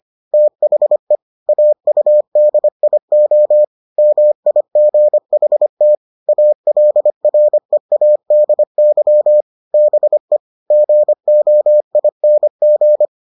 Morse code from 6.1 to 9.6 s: ALREADY

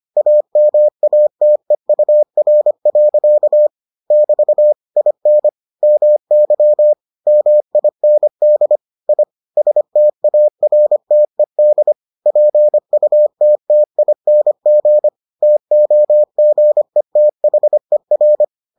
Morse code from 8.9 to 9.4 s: I

